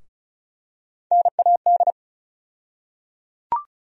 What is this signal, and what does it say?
Morse code, heard as a keyed tone, about 36 wpm, 700 Hz. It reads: NAD
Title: USA Call Sign Postfixes - 3 Letters 35wpm